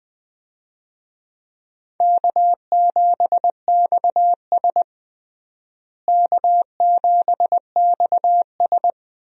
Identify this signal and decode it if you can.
Morse code, heard as a keyed tone, about 20 wpm, 700 Hz. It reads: K7XS K7XS
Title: USA Call Signs - 4 Characters - Repeated 2x - 20wpm